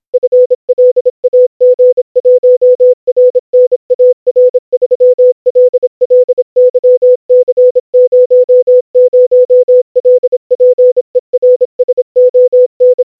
Morse code, heard as a keyed tone, about 26 wpm, 500 Hz. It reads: FLAG1RNAR3LLYC00LPERSON
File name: ctf morse / secretCopy